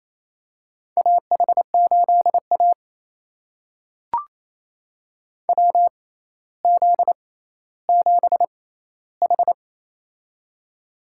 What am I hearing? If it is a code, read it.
Morse code, heard as a keyed tone, about 27 wpm, spaced as for 14 wpm, 700 Hz. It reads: AH8A WZ7H
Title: USA Call Signs - 4 Characters - 6x Character Spacing - 28wpm